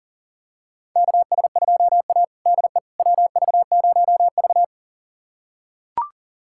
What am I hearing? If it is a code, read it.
Morse code, heard as a keyed tone, about 40 wpm, 700 Hz. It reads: KS2A DE WV0V
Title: USA Callsign DE Callsign - 4 Characters - 40wpm